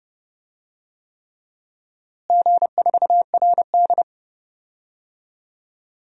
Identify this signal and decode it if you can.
Morse code, heard as a keyed tone, about 30 wpm, 700 Hz. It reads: G4RD
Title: DXCC-100 Call Signs - 4 Characters - Repeated 3x - 30wpm